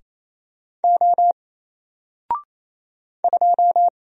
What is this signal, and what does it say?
Morse code, heard as a keyed tone, about 27 wpm, 700 Hz. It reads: O 2
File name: Single-Letter-Number-28wpm